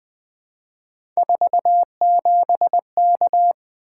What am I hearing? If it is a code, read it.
Morse code, heard as a keyed tone, about 20 wpm, 700 Hz. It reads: 47K
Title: Three Letter-Number Combinations - 20wpm